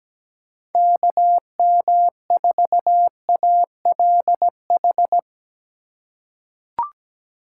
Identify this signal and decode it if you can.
Morse code, heard as a keyed tone, about 17 wpm, 700 Hz. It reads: KM4ALH
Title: USA Call Signs - 4x Character Spacing - 17wpm